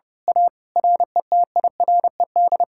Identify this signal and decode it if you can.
Morse code, heard as a keyed tone, about 28 wpm, 700 Hz. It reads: A RETIRED
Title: Speed-Racing Common QSO Phrases Encore - 30wpm